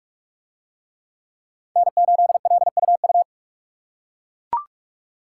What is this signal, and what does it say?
Morse code, heard as a keyed tone, about 41 wpm, 700 Hz. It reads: N8LUU